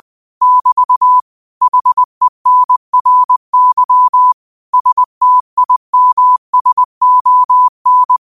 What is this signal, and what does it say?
Morse code, heard as a keyed tone, about 20 wpm, 1000 Hz. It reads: <BT> HENRY STIMSON
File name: qotdfiles / 20wpm